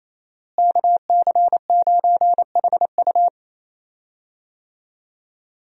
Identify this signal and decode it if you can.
Morse code, heard as a keyed tone, about 27 wpm, 700 Hz. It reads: KC9HU
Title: USA Call Signs - 5 Characters - Repeated 3x - 28wpm